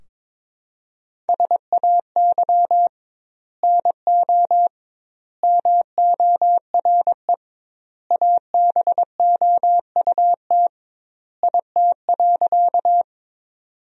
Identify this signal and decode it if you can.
Morse code, heard as a keyed tone, about 22 wpm, 700 Hz. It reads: SAY NO MORE ABOUT IT.
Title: 2x Word-Spacing - Sentences from Top 200 Words - 22wpm